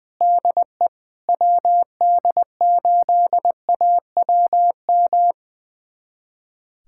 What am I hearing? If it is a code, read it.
Morse code, heard as a keyed tone, about 21 wpm, 700 Hz. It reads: DE WD8AWM
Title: USA Callsign DE Callsign 20wpm